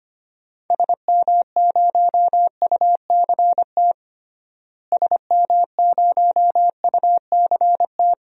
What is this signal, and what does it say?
Morse code, heard as a keyed tone, about 25 wpm, 700 Hz. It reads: SM0UCT SM0UCT
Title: DXCC-100 Call Signs - 6 Characters - Repeated 3x - 25wpm